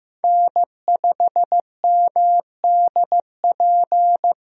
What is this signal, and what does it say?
Morse code, heard as a keyed tone, about 15 wpm, 700 Hz. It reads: N5MDP
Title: USA Call Signs - 5 Characters - Repeated 3x - 15wpm